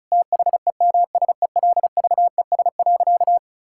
Morse code, heard as a keyed tone, about 36 wpm, 700 Hz. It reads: THEMSELVES.
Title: Sentences from Top 1000 Words - Review Part 5 - 35wpm